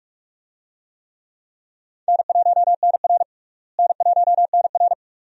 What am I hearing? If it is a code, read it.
Morse code, heard as a keyed tone, about 46 wpm, 700 Hz. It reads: N1NR N1NR
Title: USA Call Signs - 4 Characters - Repeated 2x - 45wpm